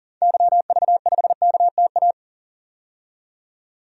Morse code, heard as a keyed tone, about 40 wpm, 700 Hz. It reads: YV5KTA
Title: DXCC-100 Call Signs - 6 Characters - 40wpm